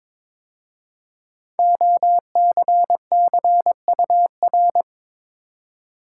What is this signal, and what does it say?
Morse code, heard as a keyed tone, about 22 wpm, 700 Hz. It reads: OCCUR
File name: 5 Letter Words - First 3 Letters Given - 22wpm